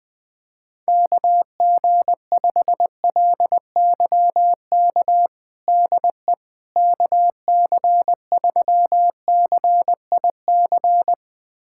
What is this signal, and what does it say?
Morse code, heard as a keyed tone, about 20 wpm, 700 Hz. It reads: KG5LYK DE KC3CIC
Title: USA Callsign DE Callsign - 6 Characters - 20wpm